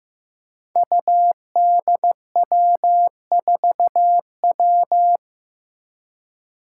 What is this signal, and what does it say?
Morse code, heard as a keyed tone, about 15 wpm, 700 Hz. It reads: UDW4W
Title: Five Challenging Letter-Number Combinations 15wpm